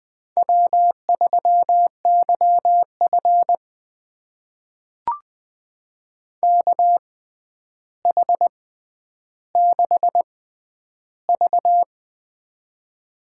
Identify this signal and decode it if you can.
Morse code, heard as a keyed tone, about 20 wpm, spaced as for 10 wpm, 700 Hz. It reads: W3YF KH6V
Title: USA Call Signs - 4 Characters - 6x Character Spacing - 20wpm